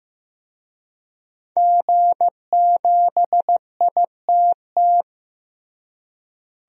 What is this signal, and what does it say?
Morse code, heard as a keyed tone, about 16 wpm, 700 Hz. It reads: G7ITT
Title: DXCC-100 Call Signs - 5 Characters - Repeated 2x - 15wpm